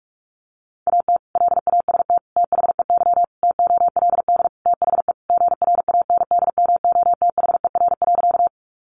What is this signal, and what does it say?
Morse code, heard as a keyed tone, about 44 wpm, 700 Hz. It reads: AT LAST THEY TOLD THE GRANDMOTHER.
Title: Sentences from Top 1200 Words 45wpm